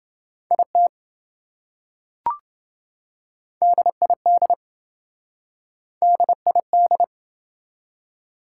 Morse code, heard as a keyed tone, about 28 wpm, 700 Hz. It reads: IT DID DID